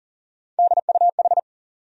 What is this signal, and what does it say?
Morse code, heard as a keyed tone, about 40 wpm, 700 Hz. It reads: DUH
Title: USA Call Sign Postfixes - 3 Letters 40wpm